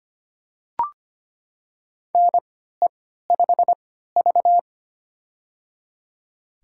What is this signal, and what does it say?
Morse code, heard as a keyed tone, about 25 wpm, spaced as for 14 wpm, 700 Hz. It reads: NE5V
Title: USA Call Signs - 3x Character Spacing - 25wpm